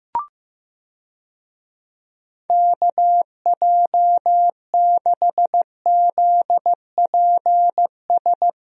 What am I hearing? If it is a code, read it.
Morse code, heard as a keyed tone, about 15 wpm, 700 Hz. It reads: KJ6ZPS
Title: USA Call Signs - 6 Characters - Repeated 2x - 15wpm